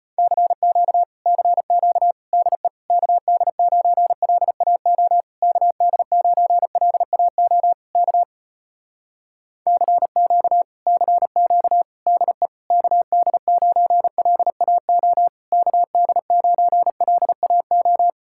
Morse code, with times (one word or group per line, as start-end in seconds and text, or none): 0.2-1.0 CQ
1.3-2.1 CQ
2.3-2.7 DE
2.9-5.2 KD9LAO
5.4-7.7 KD9LAO
8.0-8.2 K
9.7-10.6 CQ
10.9-11.8 CQ
12.1-12.5 DE
12.7-15.3 KD9LAO
15.5-18.1 KD9LAO